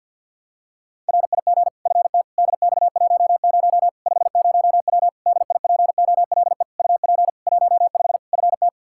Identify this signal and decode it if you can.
Morse code, heard as a keyed tone, about 49 wpm, 700 Hz. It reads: RIG FT DX10 50W DIPOLE UP 15 FT
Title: K1USN SST QSO Practice 50wpm